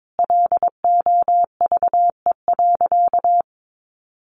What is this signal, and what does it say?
Morse code, heard as a keyed tone, about 22 wpm, 700 Hz. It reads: LOVE.
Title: Sentences from Top 400 Words 22wpm